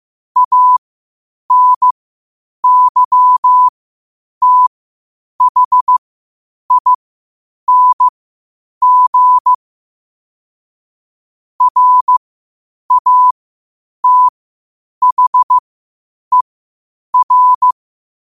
Morse code, heard as a keyed tone, about 15 wpm, spaced as for 8 wpm, 1000 Hz. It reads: ANYTHING RATHER
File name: qotdfiles / 8wpm